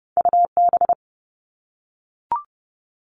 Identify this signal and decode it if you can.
Morse code, heard as a keyed tone, about 30 wpm, 700 Hz. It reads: UB
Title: All Cut Numbers - 2-Digit 30wpm